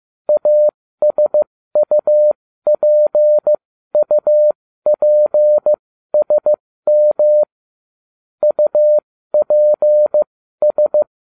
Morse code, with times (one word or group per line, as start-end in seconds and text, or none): 0.3-7.4 ASUPUPSM
8.4-11.0 UPS